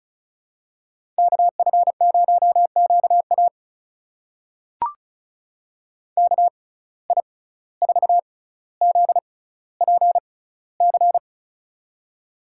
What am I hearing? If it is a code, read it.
Morse code, heard as a keyed tone, about 36 wpm, spaced as for 17 wpm, 700 Hz. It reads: KF0QA KI4ZPC
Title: USA Call Signs - 6x Character Spacing - 35wpm